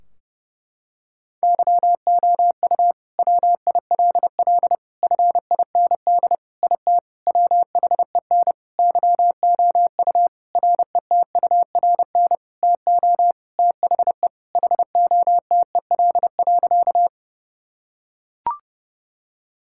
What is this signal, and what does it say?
Morse code, heard as a keyed tone, about 29 wpm, 700 Hz. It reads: YOU WILL FIND IT WHEN YOU RETURN TO THE HOTEL.
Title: Sentences from Top 1600 Words 30wpm